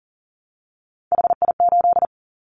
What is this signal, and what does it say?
Morse code, heard as a keyed tone, about 40 wpm, 700 Hz. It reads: HI8
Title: Callsign Prefixes for Top 100 DXCC - 40wpm